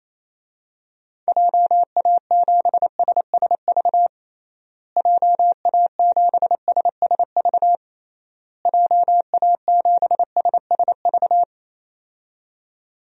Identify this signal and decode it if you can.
Morse code, heard as a keyed tone, about 28 wpm, 700 Hz. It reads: JA7SSV JA7SSV JA7SSV
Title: DXCC-100 Call Signs - 6 Characters - Repeated 3x - 28wpm